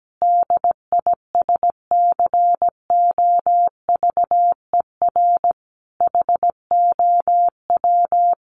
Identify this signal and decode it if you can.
Morse code, heard as a keyed tone, about 17 wpm, 700 Hz. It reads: DISCOVER HOW
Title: Sentences from Top 1100 Words 17wpm